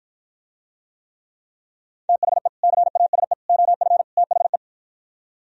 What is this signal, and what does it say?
Morse code, heard as a keyed tone, about 52 wpm, 700 Hz. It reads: THE CASE OF THE